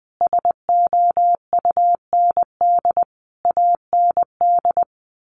Morse code, heard as a keyed tone, about 19 wpm, 700 Hz. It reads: SOUND AND
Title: Sentences from Top 1400 Words 20wpm